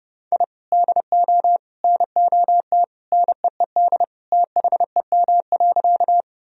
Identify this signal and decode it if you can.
Morse code, heard as a keyed tone, about 32 wpm, 700 Hz. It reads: I DO NOT NEED THEM.
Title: Sentences from Top 1000 Words - Review Part 1 - 30wpm